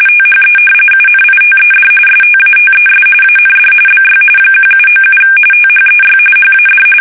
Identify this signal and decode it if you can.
RTTY, about 45.45 baud. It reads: THE UNITED STATES AND BY 1797 THEY HAD
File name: fsk850 / usnavy1